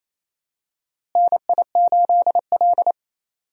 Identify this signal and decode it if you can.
Morse code, heard as a keyed tone, about 28 wpm, 700 Hz. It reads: NI8L
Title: USA Call Signs - 4 Characters - Repeated 3x - 28wpm